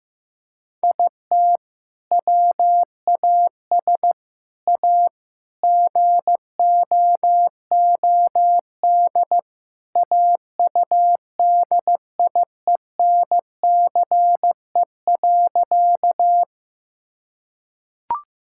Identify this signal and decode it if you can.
Morse code, heard as a keyed tone, about 15 wpm, 700 Hz. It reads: IT WAS A GOOD AUDIENCE.
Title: Sentences from Top 1000 Words - Review Part 5 - 15wpm